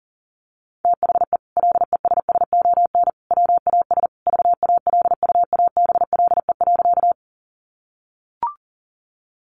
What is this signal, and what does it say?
Morse code, heard as a keyed tone, about 40 wpm, 700 Hz. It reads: THE LESSON WAS VALUABLE.
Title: Sentences from Top 1200 Words 40wpm